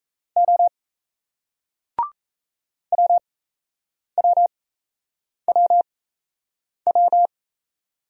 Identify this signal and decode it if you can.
Morse code, heard as a keyed tone, about 38 wpm, spaced as for 8 wpm, 700 Hz. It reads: O WWWW